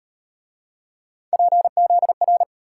Morse code, heard as a keyed tone, about 36 wpm, 700 Hz. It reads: PZR